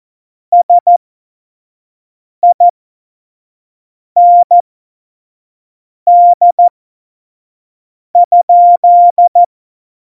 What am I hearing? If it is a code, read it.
Morse code, heard as a keyed tone, about 14 wpm, spaced as for 5 wpm, 700 Hz. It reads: SIND?
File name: Morsekurs / Witze / Witz 19 Mai 2020, 25 F70